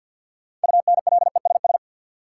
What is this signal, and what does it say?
Morse code, heard as a keyed tone, about 52 wpm, 700 Hz. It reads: UNLESS
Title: Sentences from Top 1300 Words - New Word Drill Alt - 50wpm